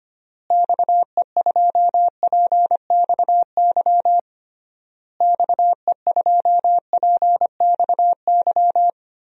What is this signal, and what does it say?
Morse code, heard as a keyed tone, about 25 wpm, 700 Hz. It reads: XE2PXY XE2PXY
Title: DXCC-100 Call Signs - 6 Characters - Repeated 3x - 25wpm